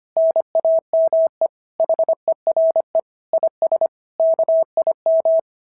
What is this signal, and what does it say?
Morse code, heard as a keyed tone, about 24 wpm, 650 Hz. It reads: NAME HERE IS KIM
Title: SOTA QSO Practice 25wpm